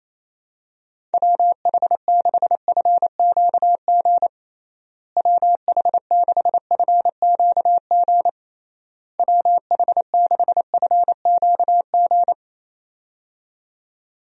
Morse code, heard as a keyed tone, about 28 wpm, 700 Hz. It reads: WH6FQG WH6FQG WH6FQG
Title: USA Call Signs - 6 Characters - Repeated 3x - 28wpm